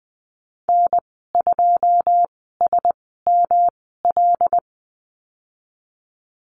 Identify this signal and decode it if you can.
Morse code, heard as a keyed tone, about 20 wpm, spaced as for 15 wpm, 700 Hz. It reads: N2SML
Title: 128 Lesson 21g - Number 6 - Call Signs 20wpm